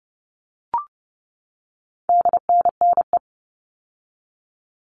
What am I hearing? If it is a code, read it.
Morse code, heard as a keyed tone, about 30 wpm, 700 Hz. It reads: DNNE
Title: All Cut Numbers - 4-Digit 30wpm